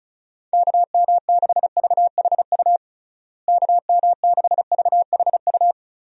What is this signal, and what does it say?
Morse code, heard as a keyed tone, about 34 wpm, 700 Hz. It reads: KM6VHU KM6VHU
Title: USA Call Signs - 6 Characters - Repeated 3x - 35wpm